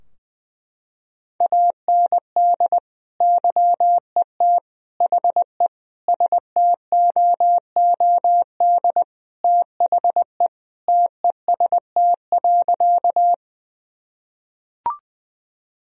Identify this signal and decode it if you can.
Morse code, heard as a keyed tone, about 20 wpm, 700 Hz. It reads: AND YET HE STOOD THE TEST.
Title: Sentences from Top 1000 Words - Review Part 3 - 20wpm